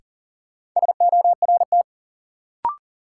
Morse code, heard as a keyed tone, about 40 wpm, 700 Hz. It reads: SORT
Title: Sets of 2 Words - New Word Drill - 40wpm